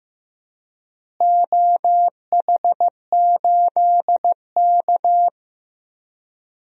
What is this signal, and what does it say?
Morse code, heard as a keyed tone, about 15 wpm, 700 Hz. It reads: OH8K
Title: DXCC-100 Call Signs - 4 Characters - Repeated 3x - 15wpm